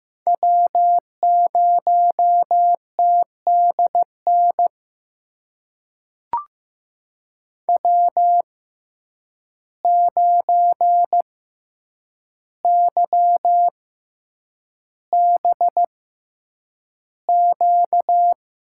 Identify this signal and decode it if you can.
Morse code, heard as a keyed tone, about 15 wpm, spaced as for 8 wpm, 700 Hz. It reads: W0TDN W9YBQ
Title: USA Call Signs - 6x Character Spacing - 15wpm